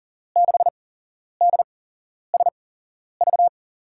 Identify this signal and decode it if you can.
Morse code, heard as a keyed tone, about 40 wpm, spaced as for 11 wpm, 700 Hz. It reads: 6DSV
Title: USA Call Signs - 6 Characters - 8x Character Spacing - 40wpm